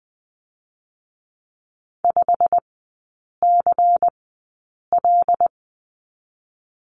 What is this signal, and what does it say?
Morse code, heard as a keyed tone, about 20 wpm, spaced as for 8 wpm, 700 Hz. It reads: 5CL
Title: 127 Lesson 21e - Number 6 - Three Characters 20wpm